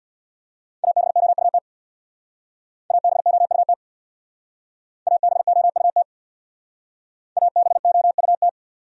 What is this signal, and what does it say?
Morse code, heard as a keyed tone, about 64 wpm, 700 Hz. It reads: ABOUT ABOUT ABOUT ABOUT